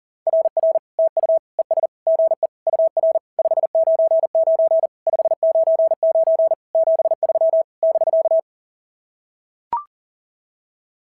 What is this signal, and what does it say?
Morse code, heard as a keyed tone, about 40 wpm, 690 Hz. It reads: RR TU ES GE UR 599 599 73 <BK>
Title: SOTA QSO Practice 40wpm